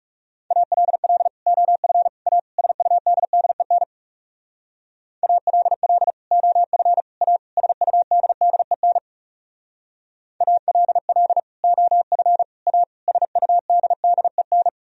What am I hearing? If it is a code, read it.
Morse code, heard as a keyed tone, about 40 wpm, 700 Hz. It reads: ALL OF A SUDDEN ALL OF A SUDDEN ALL OF A SUDDEN